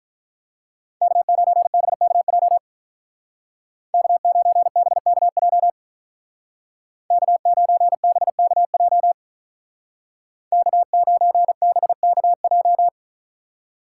Morse code, heard as a keyed tone, about 44 wpm, 700 Hz. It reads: K9BKJ K9BKJ K9BKJ K9BKJ